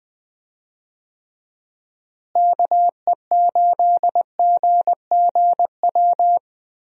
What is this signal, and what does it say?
Morse code, heard as a keyed tone, about 21 wpm, 700 Hz. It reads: KE8GGW
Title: USA Call Signs - 6 Characters - Repeated 3x - 20wpm